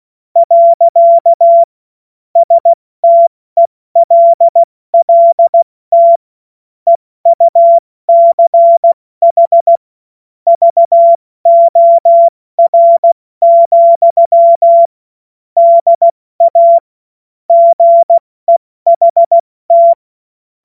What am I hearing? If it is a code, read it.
Morse code, heard as a keyed tone, about 16 wpm, 680 Hz. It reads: . STELLT EUCH VOR, DA GEHT
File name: Morsekurs / Witze / Witz 02 April 2024 70 F80